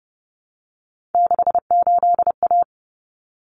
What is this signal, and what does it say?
Morse code, heard as a keyed tone, about 30 wpm, 700 Hz. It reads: 68A
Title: Common Cut Numbers - 3-Digit 30wpm